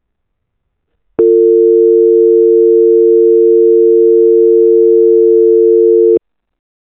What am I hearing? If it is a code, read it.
Dial tone (UK), 350 + 450 Hz.